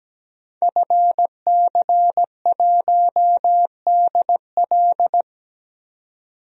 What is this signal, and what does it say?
Morse code, heard as a keyed tone, about 17 wpm, 700 Hz. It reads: FC1DL
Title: DXCC-100 Call Signs - 5 Characters - 17wpm